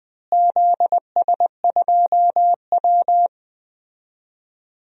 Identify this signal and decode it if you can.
Morse code, heard as a keyed tone, about 21 wpm, 700 Hz. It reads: ZS2W
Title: DXCC-100 Call Signs - 4 Characters - Repeated 2x - 20wpm